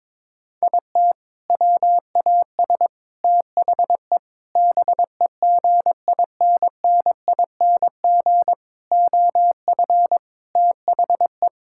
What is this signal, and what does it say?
Morse code, heard as a keyed tone, about 22 wpm, 700 Hz. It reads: IT WAS THE BEGINNING OF THE
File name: practice / 22wpm / Sentences with 4-Word Phrases - 22wpm